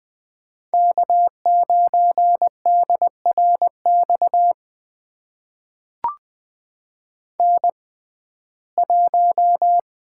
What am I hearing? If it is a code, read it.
Morse code, heard as a keyed tone, about 21 wpm, spaced as for 15 wpm, 700 Hz. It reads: K9DRX N1
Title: USA Call Signs - 5 Characters - 6x Character Spacing - 20wpm